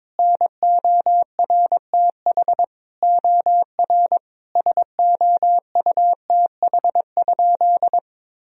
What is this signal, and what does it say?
Morse code, heard as a keyed tone, about 22 wpm, 700 Hz. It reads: NORTH OR SOUTH?
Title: Sentences from Top 1000 Words - Review Part 2 - 22wpm